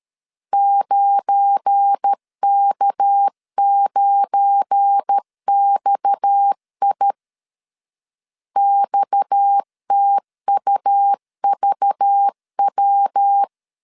Morse code, heard as a keyed tone, about 13 wpm, 800 Hz. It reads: 9K9XI XTUVW
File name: LEZIONI / 7 GF9Z / 7 GF9 12wpm 2319